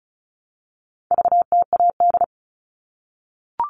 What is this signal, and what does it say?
Morse code, heard as a keyed tone, about 34 wpm, 700 Hz. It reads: VTAD
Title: All Cut Numbers - 4-Digit 35wpm